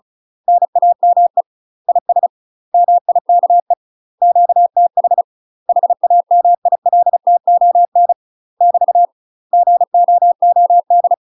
Morse code, heard as a keyed tone, about 36 wpm, 700 Hz. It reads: NAME IS MIKE QTH HAMILTON <BT> GOOD
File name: UK MASTER QSOs TRAINING / 35WPM / UK MASTER QSOs 0003